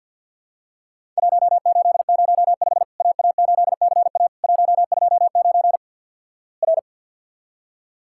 Morse code, heard as a keyed tone, about 50 wpm, 700 Hz. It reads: 1805 AA8CA 129 R
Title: Worked All Europe DX Contest - QTC Exchange - 50wpm